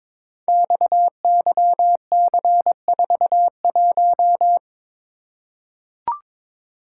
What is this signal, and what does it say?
Morse code, heard as a keyed tone, about 22 wpm, 700 Hz. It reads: XYC41
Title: Code Groups - 5 Letter:Numbers - 22wpm